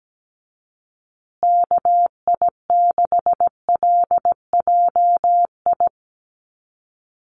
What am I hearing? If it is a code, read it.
Morse code, heard as a keyed tone, about 17 wpm, 700 Hz. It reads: KI6LJI
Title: USA Call Signs Repeated 2x 17wpm